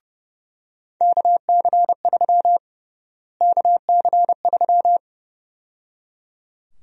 Morse code, heard as a keyed tone, about 30 wpm, 700 Hz. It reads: KC3 KC3
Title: USA Call Sign Prefixes - Repeated 2x - 30wpm